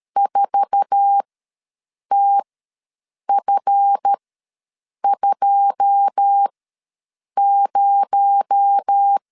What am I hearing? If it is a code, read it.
Morse code, heard as a keyed tone, about 13 wpm, spaced as for 7 wpm, 800 Hz. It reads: 4TF20
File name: LEZIONI / 9 2 ws 12wpm 3406